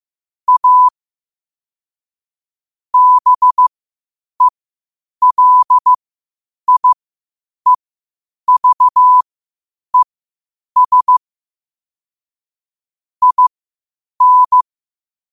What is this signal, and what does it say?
Morse code, heard as a keyed tone, about 15 wpm, spaced as for 8 wpm, 1000 Hz. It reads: A BELIEVES IN
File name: qotdfiles / 8wpm